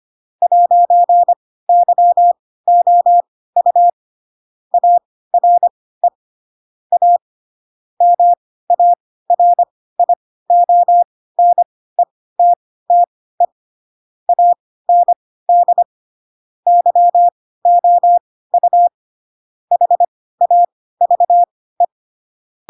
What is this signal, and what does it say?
Morse code, heard as a keyed tone, about 25 wpm, spaced as for 16 wpm, 700 Hz. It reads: 'YOU ARE A MARIONETTE AND YOU HAVE
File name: Pinocchio / Pinocchio ch7 10 25 ik1hge